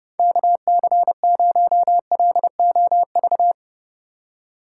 Morse code, heard as a keyed tone, about 30 wpm, 700 Hz. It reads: KC0LOV
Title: USA Call Signs - 6 Characters - 2x Character Spacing - 30wpm